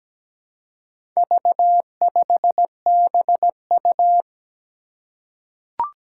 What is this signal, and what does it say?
Morse code, heard as a keyed tone, about 17 wpm, 700 Hz. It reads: V5BU
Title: Four Challenging Letter-Number Combinations 17wpm